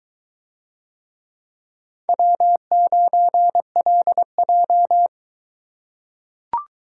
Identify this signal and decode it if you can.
Morse code, heard as a keyed tone, about 22 wpm, 700 Hz. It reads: W9LJ